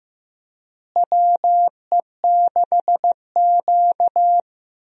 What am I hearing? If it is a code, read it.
Morse code, heard as a keyed tone, about 15 wpm, 700 Hz. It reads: WE6Q
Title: USA Call Signs - 4 Characters - Repeated 2x - 15wpm